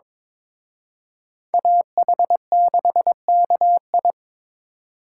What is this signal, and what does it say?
Morse code, heard as a keyed tone, about 26 wpm, 700 Hz. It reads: AH6KI
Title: Speed Racing - USA 2x2 Call Signs 22wpm